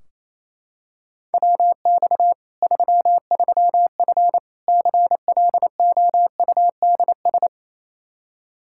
Morse code, heard as a keyed tone, about 28 wpm, 700 Hz. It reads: WX 33F CLOUDS
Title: Practice QSO - Exchange 2 - Info - 28wpm